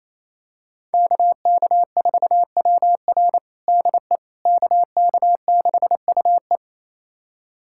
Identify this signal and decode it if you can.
Morse code, heard as a keyed tone, about 28 wpm, 700 Hz. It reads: KK4WR DE KK6UE
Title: USA Callsign DE Callsign - 5 Characters - 28wpm